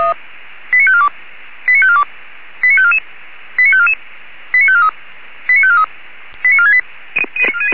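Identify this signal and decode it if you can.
Five-tone selective call (ZVEI), heard as quick runs of five tones, tones of about 70 ms. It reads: 89542 89542 89544 89544 89543 89543 89468
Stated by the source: ZVEI five-tone selective call received with an SDR, archived by the Feuerwehr Orlamünde fire brigade.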